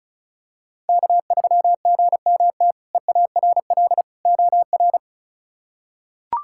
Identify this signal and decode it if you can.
Morse code, heard as a keyed tone, about 36 wpm, 700 Hz. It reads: K3GMT EARL OR
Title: K1USN SST QSO Practice - 35wpm